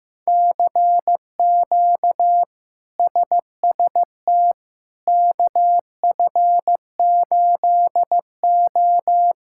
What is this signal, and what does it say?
Morse code, heard as a keyed tone, about 16 wpm, 700 Hz. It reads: CQ SST KF8O
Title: K1USN SST QSO Practice - 15wpm